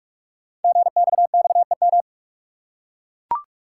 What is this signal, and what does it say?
Morse code, heard as a keyed tone, about 44 wpm, 700 Hz. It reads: GXXEM
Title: Code Groups - 5 Letters - 45wpm